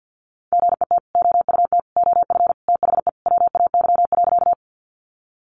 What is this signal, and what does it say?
Morse code, heard as a keyed tone, about 50 wpm, 700 Hz. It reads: GET OUT OF THE WAY.
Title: Sentences from Top 200 Words 50wpm